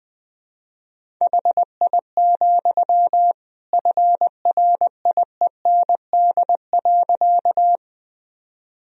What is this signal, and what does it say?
Morse code, heard as a keyed tone, about 21 wpm, 700 Hz. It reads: HI, FRIEND.
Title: Sentences from Top 1000 Words - Review Part 3 - 20wpm